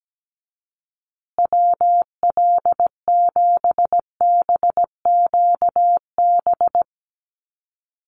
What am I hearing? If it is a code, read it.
Morse code, heard as a keyed tone, about 17 wpm, 700 Hz. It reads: WL7BQB
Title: USA Call Signs Repeated 3x 17wpm